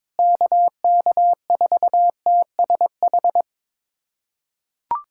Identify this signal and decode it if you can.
Morse code, heard as a keyed tone, about 22 wpm, 700 Hz. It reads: KK4TSH
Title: USA Callsign DE Callsign - 6 Characters - 22wpm